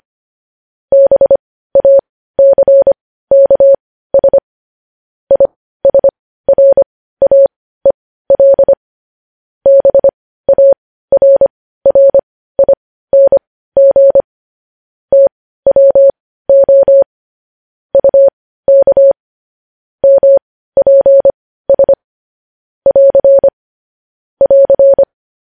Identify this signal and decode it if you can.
Morse code, heard as a keyed tone, about 25 wpm, spaced as for 15 wpm, 550 Hz. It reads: BACKS ISRAEL BARRING TWO UK MPS <AR> <AR>
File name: podcast / News headlines-1x-25x15